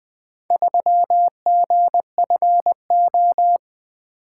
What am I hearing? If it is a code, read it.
Morse code, heard as a keyed tone, about 24 wpm, 700 Hz. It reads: 3GFO
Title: Speed-Racing - Four Letter-Number Combinations - 20wpm